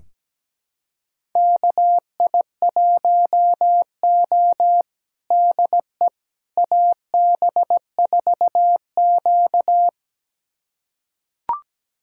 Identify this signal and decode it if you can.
Morse code, heard as a keyed tone, about 17 wpm, 700 Hz. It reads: KI1O DE AB4Q
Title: USA Callsign DE Callsign - 4 Characters - 17wpm